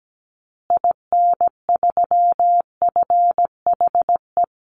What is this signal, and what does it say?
Morse code, heard as a keyed tone, about 17 wpm, 700 Hz. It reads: IN3FHE
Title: Call Signs Repeated 2x - 17wpm